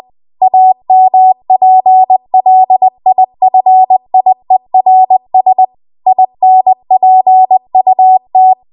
Morse code, heard as a keyed tone, about 20 wpm, 750 Hz. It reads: AMPLIFIERS INPUT
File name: Archive / 20 WPM / 20WPM